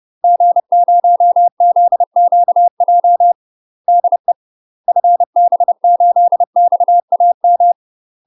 Morse code, heard as a keyed tone, about 30 wpm, 700 Hz. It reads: G0ZQJ DE FB8XAM
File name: UK MASTER QSOs TRAINING / 30WPM / UK MASTER QSOs 0019